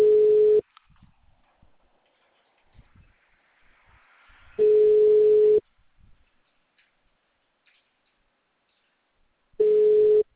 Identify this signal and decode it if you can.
Ringback tone (Europe, 425 Hz).
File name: out- -2014- - - .0